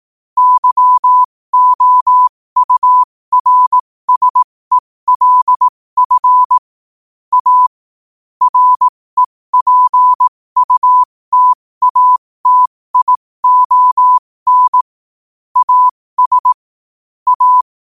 Morse code, heard as a keyed tone, about 18 wpm, spaced as for 15 wpm, 1000 Hz. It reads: YOURSELF A REPUTATION AS A
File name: qotdfiles / 15wpm